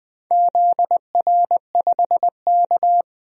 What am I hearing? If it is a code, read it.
Morse code, heard as a keyed tone, about 21 wpm, 700 Hz. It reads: ZR5K
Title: DXCC-100 Call Signs - 4 Characters - 20wpm